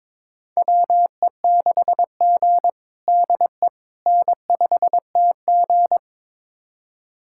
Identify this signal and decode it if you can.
Morse code, heard as a keyed tone, about 22 wpm, 700 Hz. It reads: WE6G DE N5TG
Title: USA Callsign DE Callsign - 4 Characters - 22wpm